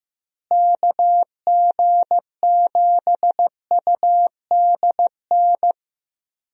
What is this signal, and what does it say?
Morse code, heard as a keyed tone, about 15 wpm, 700 Hz. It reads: KG7UDN
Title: USA Call Signs - 6 Characters - Repeated 3x - 15wpm